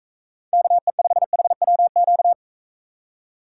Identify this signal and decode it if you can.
Morse code, heard as a keyed tone, about 43 wpm, 700 Hz. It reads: KE5HWQ